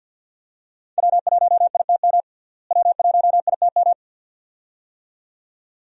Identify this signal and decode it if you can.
Morse code, heard as a keyed tone, about 52 wpm, 700 Hz. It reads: W1ITM W1ITM
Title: USA Call Signs - 5 Characters - Repeated 2x - 50wpm